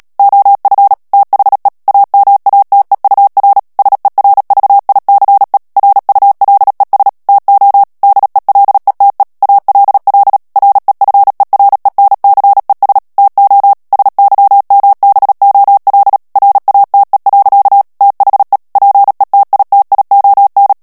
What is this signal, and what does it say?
Morse code, heard as a keyed tone, about 37 wpm, 800 Hz. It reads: OF THE AMATEUR SERVICE RULES TO DELETE ALL REFERENCES TO SYMBOL RATE. THE PETITION
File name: morse mp3 / ANGLAIS texte clair 3 vitesse 35 mots minute